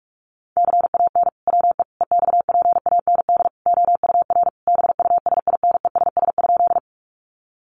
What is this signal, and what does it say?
Morse code, heard as a keyed tone, about 44 wpm, 700 Hz. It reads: CAN WE EXPAND OUR BUSINESS?
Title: Sentences from Top 1400 Words 45wpm